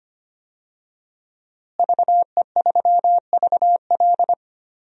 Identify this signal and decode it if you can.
Morse code, heard as a keyed tone, about 25 wpm, 700 Hz. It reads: VE3VL
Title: DXCC-100 Call Signs - 5 Characters - Repeated 2x - 25wpm